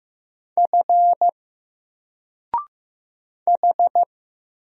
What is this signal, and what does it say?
Morse code, heard as a keyed tone, about 15 wpm, 700 Hz. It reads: F H
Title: Single Character (Letter:Number:Punctuation:Prosign) 15wpm